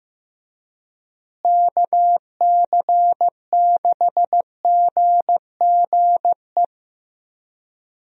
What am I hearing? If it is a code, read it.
Morse code, heard as a keyed tone, about 16 wpm, 700 Hz. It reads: KC6GGE
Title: USA Call Signs - 6 Characters - Repeated 3x - 15wpm